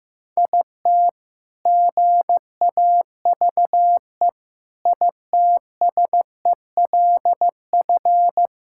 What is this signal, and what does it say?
Morse code, heard as a keyed tone, about 15 wpm, 700 Hz. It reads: IT GAVE ITSELF
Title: Sentences from Top 1000 Words - Review Part 5 - 15wpm